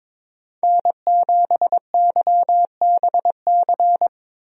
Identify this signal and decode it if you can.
Morse code, heard as a keyed tone, about 21 wpm, 700 Hz. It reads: N7YBC
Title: USA Call Signs - 5 Characters - 22wpm